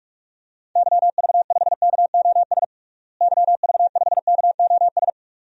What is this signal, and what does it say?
Morse code, heard as a keyed tone, about 46 wpm, 700 Hz. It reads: YV5KOS YV5KOS
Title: DXCC-100 Call Signs - 6 Characters - Repeated 2x - 45wpm